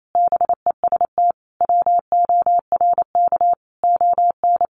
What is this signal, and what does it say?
Morse code, heard as a keyed tone, about 28 wpm, 700 Hz. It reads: BEST WORK ON
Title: Sentences from Top 1200 Words 28wpm